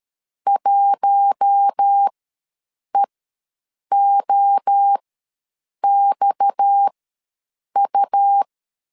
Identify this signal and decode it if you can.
Morse code, heard as a keyed tone, about 13 wpm, spaced as for 7 wpm, 800 Hz. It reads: 1EOXU
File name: LEZIONI / 6 PKX0 / 6 PKX ws 12wpm 3106